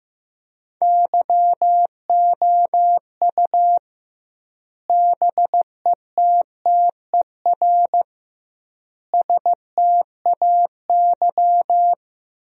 Morse code, heard as a keyed tone, about 15 wpm, 700 Hz. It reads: YOU BETTER STAY
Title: 2x Word-Spacing - Sentences from Top 500 Words - 15wpm